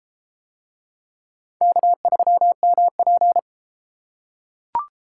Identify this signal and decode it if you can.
Morse code, heard as a keyed tone, about 30 wpm, 700 Hz. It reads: K3MP